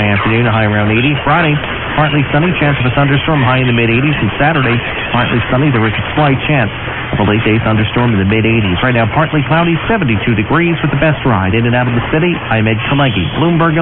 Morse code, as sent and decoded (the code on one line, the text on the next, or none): none